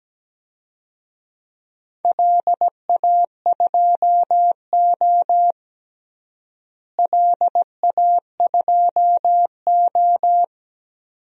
Morse code, heard as a keyed tone, about 17 wpm, 700 Hz. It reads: LA2O LA2O
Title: DXCC-100 Call Signs - 4 Characters - Repeated 2x - 17wpm